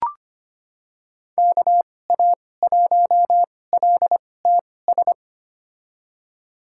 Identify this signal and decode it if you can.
Morse code, heard as a keyed tone, about 25 wpm, spaced as for 18 wpm, 700 Hz. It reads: KA1LTS
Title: USA Call Signs - 6 Characters - 2x Character Spacing - 25wpm